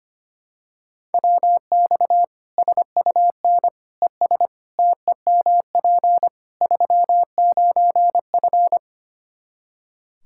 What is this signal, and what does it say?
Morse code, heard as a keyed tone, about 25 wpm, 700 Hz. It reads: WX SUN ES TEMP 39F
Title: Practice QSO - Exchange 2 - Info - 25wpm